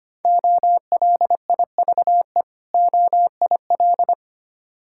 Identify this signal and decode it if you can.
Morse code, heard as a keyed tone, about 25 wpm, 700 Hz. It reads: OLIVE OIL
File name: practice / 25wpm / Difficult Words with Hints - 25wpm